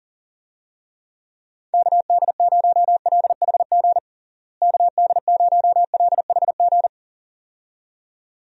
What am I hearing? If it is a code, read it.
Morse code, heard as a keyed tone, about 37 wpm, 700 Hz. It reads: KD0LHG KD0LHG
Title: USA Call Signs - 6 Characters - Repeated 2x - 40wpm